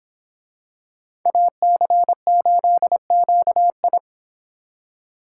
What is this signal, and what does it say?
Morse code, heard as a keyed tone, about 24 wpm, 700 Hz. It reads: AC8QI